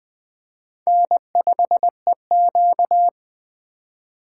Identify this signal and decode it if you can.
Morse code, heard as a keyed tone, about 20 wpm, 700 Hz. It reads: N5EQ